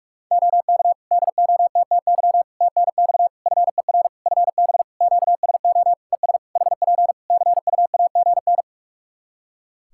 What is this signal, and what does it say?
Morse code, heard as a keyed tone, about 43 wpm, 700 Hz. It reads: OK DOTTY TNX FER FB QSO ES HP CUAGN
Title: Practice QSO - Exchange 4 - Ending - 45wpm